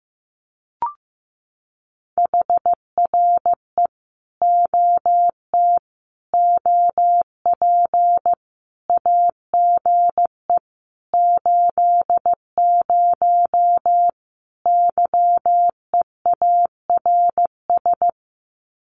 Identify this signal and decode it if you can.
Morse code, heard as a keyed tone, about 15 wpm, 700 Hz. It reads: HRE OT OP AGE 80 YEARS
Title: Common QSO Phrases 15wpm